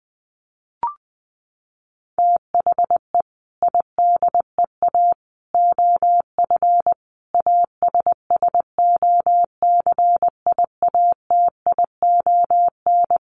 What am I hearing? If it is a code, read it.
Morse code, heard as a keyed tone, about 20 wpm, 700 Hz. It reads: THE IDEA OF ASSOCIATION
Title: Sentences from Top 1100 Words 20wpm